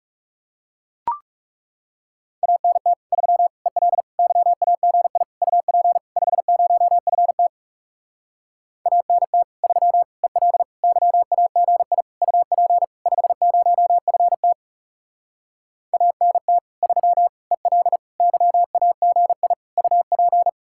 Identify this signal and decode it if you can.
Morse code, heard as a keyed tone, about 37 wpm, 700 Hz. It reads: ANT 3 EL YAGI UP 50FT ANT 3 EL YAGI UP 50FT ANT 3 EL YAGI UP
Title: Speed-Racing Common QSO Phrases - 30wpm